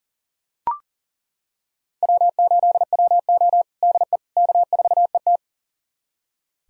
Morse code, heard as a keyed tone, about 40 wpm, 700 Hz. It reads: W8WO DE K4ET
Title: USA Callsign DE Callsign - 4 Characters - 40wpm